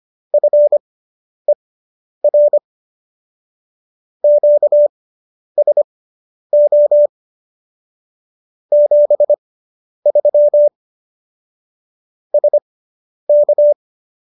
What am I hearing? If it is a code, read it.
Morse code, heard as a keyed tone, about 25 wpm, spaced as for 10 wpm, 590 Hz. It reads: FER QSO 73 SK